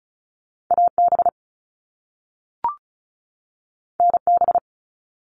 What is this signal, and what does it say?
Morse code, heard as a keyed tone, about 36 wpm, 700 Hz. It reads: AB NB
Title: Canadian Provinces and Territories 35wpm